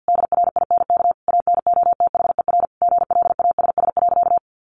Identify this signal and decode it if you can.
Morse code, heard as a keyed tone, about 50 wpm, 700 Hz. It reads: DRINK ANOTHER GLASS.
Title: Sentences from Top 1500 Words 50wpm